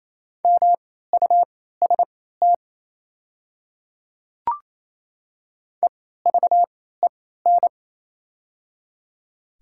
Morse code, heard as a keyed tone, about 27 wpm, spaced as for 16 wpm, 700 Hz. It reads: MUST EVEN
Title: 3x Character Spacing - Top 300 Words - 28wpm